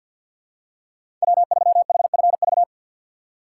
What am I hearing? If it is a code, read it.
Morse code, heard as a keyed tone, about 52 wpm, 700 Hz. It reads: W3HFV
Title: USA Call Signs - 5 Characters - 3x Character Spacing - 50wpm